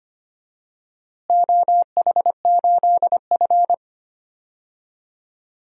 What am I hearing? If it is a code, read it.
Morse code, heard as a keyed tone, about 25 wpm, 700 Hz. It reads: OH8F
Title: DXCC-100 Call Signs - 4 Characters - Repeated 3x - 25wpm